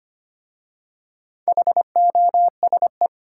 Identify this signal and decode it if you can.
Morse code, heard as a keyed tone, about 25 wpm, 700 Hz. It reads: HOSE
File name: practice / 25wpm / Easy Words with Hints - 25wpm